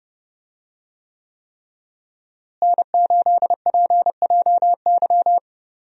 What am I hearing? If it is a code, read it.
Morse code, heard as a keyed tone, about 30 wpm, 700 Hz. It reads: N8PJY
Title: USA Call Signs - 5 Characters - Repeated 3x - 30wpm